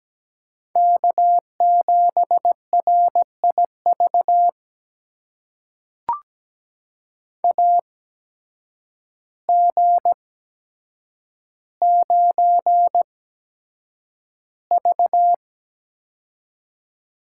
Morse code, heard as a keyed tone, about 17 wpm, spaced as for 8 wpm, 700 Hz. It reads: K7RIV AG9V
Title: USA Call Signs - 8x Character Spacing - 17wpm